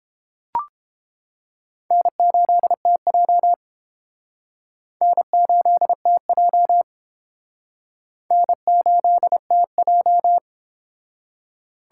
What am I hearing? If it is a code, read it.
Morse code, heard as a keyed tone, about 30 wpm, 700 Hz. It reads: N8TJ N8TJ N8TJ